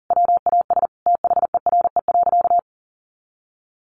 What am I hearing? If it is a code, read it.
Morse code, heard as a keyed tone, about 40 wpm, 700 Hz. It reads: WAS THERE.
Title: A Sound Of Thunder 40wpm